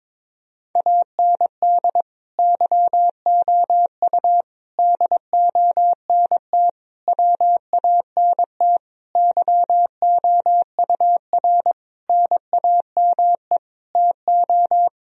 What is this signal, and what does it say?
Morse code, heard as a keyed tone, about 22 wpm, 700 Hz. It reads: AND YOU DONT WANT YOUR NAME TO
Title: Sentences from Top 1000 Words - Review Part 2 - 22wpm